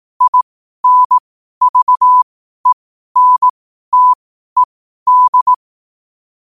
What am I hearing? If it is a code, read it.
Morse code, heard as a keyed tone, about 18 wpm, spaced as for 12 wpm, 1000 Hz. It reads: INVENTED
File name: qotdfiles / 12wpm